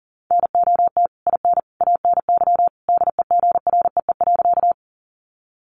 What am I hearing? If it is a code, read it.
Morse code, heard as a keyed tone, about 40 wpm, 700 Hz. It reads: NOT IN ANY DEGREE.
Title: Sentences from Top 1500 Words 40wpm